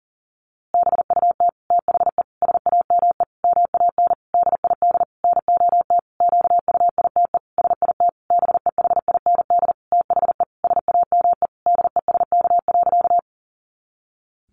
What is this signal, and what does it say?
Morse code, heard as a keyed tone, about 37 wpm, 700 Hz. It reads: BUT THE SAME MAN DID NOT QUITE SIT BEHIND THE SAME DESK.
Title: A Sound Of Thunder 40wpm